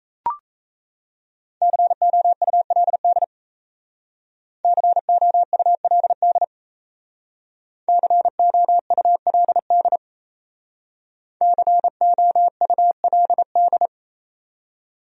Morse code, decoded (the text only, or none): COULD COULD COULD COULD